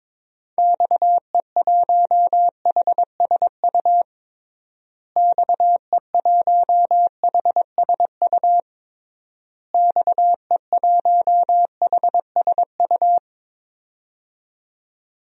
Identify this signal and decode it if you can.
Morse code, heard as a keyed tone, about 22 wpm, 700 Hz. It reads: XE1HSU XE1HSU XE1HSU
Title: DXCC-100 Call Signs - 6 Characters - Repeated 3x - 22wpm